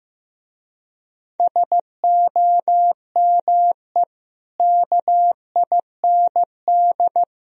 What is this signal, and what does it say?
Morse code, heard as a keyed tone, about 15 wpm, 700 Hz. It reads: SOME KIND
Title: Speed-Racing - Sets of 3 Words 15wpm